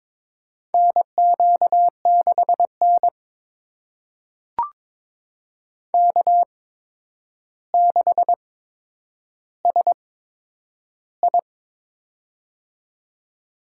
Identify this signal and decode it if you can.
Morse code, heard as a keyed tone, about 22 wpm, spaced as for 9 wpm, 700 Hz. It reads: NQ6N K6SI
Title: USA Call Signs - 8x Character Spacing - 22wpm